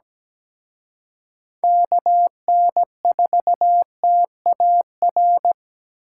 Morse code, heard as a keyed tone, about 20 wpm, 700 Hz. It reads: KN4TAR
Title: Speed Racing - USA 2x3 Call Signs 17wpm